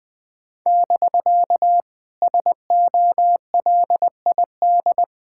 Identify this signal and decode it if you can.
Morse code, heard as a keyed tone, about 20 wpm, 700 Hz. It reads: <BK> SOLID
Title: Speed-Racing Common QSO Phrases - 17wpm